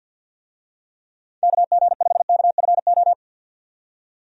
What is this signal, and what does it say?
Morse code, heard as a keyed tone, about 50 wpm, 700 Hz. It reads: KG5KFO
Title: USA Call Signs - 6 Characters - 50wpm